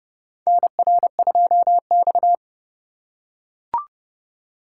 Morse code, heard as a keyed tone, about 32 wpm, 700 Hz. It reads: NR2X
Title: USA Call Signs - 4 Characters - 3x Character Spacing - 30wpm